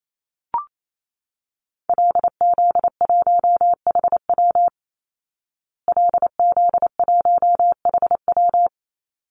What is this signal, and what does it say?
Morse code, heard as a keyed tone, about 28 wpm, 700 Hz. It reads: LZ1HW LZ1HW
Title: Call Signs Repeated 3x - 28wpm